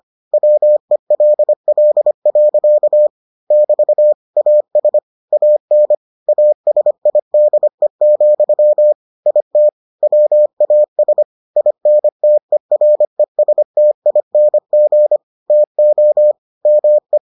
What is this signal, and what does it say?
Morse code, heard as a keyed tone, about 25 wpm, 600 Hz. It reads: WELL. <BT> AS AN ASIDE, IT WAS INTERESTING TO ME